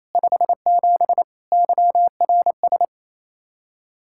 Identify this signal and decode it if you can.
Morse code, heard as a keyed tone, about 37 wpm, 700 Hz. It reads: 57 YRS